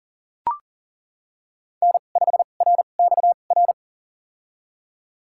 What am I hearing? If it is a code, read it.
Morse code, heard as a keyed tone, about 37 wpm, spaced as for 28 wpm, 700 Hz. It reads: N5RXR
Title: USA Call Signs - 5 Characters - 2x Character Spacing - 40wpm